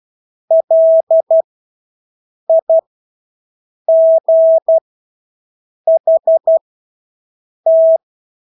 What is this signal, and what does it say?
Morse code, heard as a keyed tone, about 12 wpm, spaced as for 6 wpm, 650 Hz. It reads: LIGHT
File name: Engelske ord 12-6 wpm